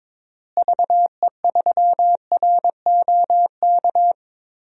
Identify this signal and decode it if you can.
Morse code, heard as a keyed tone, about 22 wpm, 700 Hz. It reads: VE3ROK
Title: DXCC-100 Call Signs - 6 Characters - Repeated 2x - 22wpm